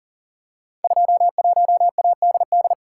Morse code, heard as a keyed tone, about 37 wpm, 700 Hz. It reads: 21ADD
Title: Code Groups - 5 Letter:Numbers - 40wpm